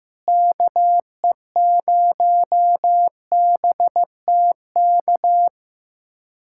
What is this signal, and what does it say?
Morse code, heard as a keyed tone, about 15 wpm, 700 Hz. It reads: KE0BTK
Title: USA Callsign DE Callsign - 6 Characters - 15wpm